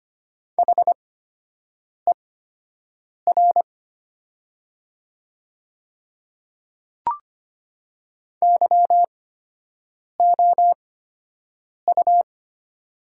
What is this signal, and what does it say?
Morse code, heard as a keyed tone, about 25 wpm, spaced as for 7 wpm, 700 Hz. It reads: HER YOU
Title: 8x Character Spacing - Top 100 Words - 25wpm